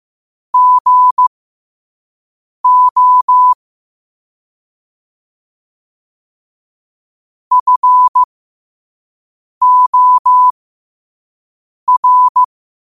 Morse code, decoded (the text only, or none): GO FOR